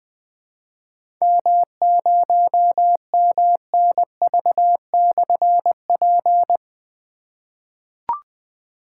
Morse code, heard as a keyed tone, about 20 wpm, 700 Hz. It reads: M0MNV/P
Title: Difficult Call Signs 20wpm